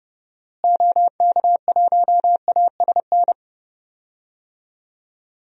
Morse code, heard as a keyed tone, about 29 wpm, 700 Hz. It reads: OK1ASN
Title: DXCC-100 Call Signs - 6 Characters - Repeated 3x - 30wpm